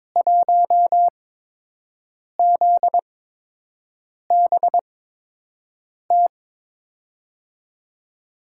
Morse code, heard as a keyed tone, about 22 wpm, spaced as for 6 wpm, 700 Hz. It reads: 1ZBT
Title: USA Call Signs - 5 Characters - 8x Character Spacing - 22wpm